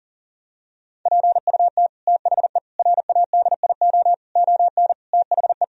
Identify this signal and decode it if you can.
Morse code, heard as a keyed tone, about 40 wpm, 700 Hz. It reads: PUT THE RADIO ON THE
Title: Sentences from Top 1000 Words - Review Part 3 - 40wpm